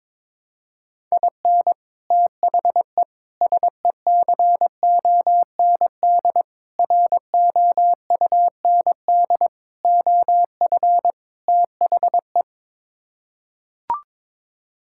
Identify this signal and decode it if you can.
Morse code, heard as a keyed tone, about 22 wpm, 700 Hz. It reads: IN THE SECOND ROUND OF THE
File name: practice / 22wpm / Sets of 6 Words - 22wpm